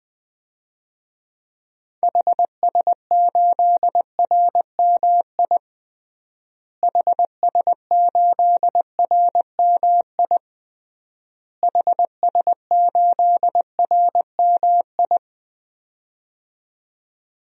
Morse code, heard as a keyed tone, about 20 wpm, 700 Hz. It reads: HS8RMI HS8RMI HS8RMI
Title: DXCC-100 Call Signs - 6 Characters - Repeated 3x - 20wpm